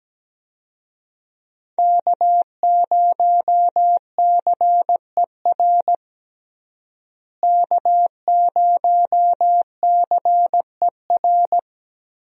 Morse code, heard as a keyed tone, about 17 wpm, 700 Hz. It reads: K0CER K0CER
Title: USA Call Signs - 5 Characters - Repeated 2x - 17wpm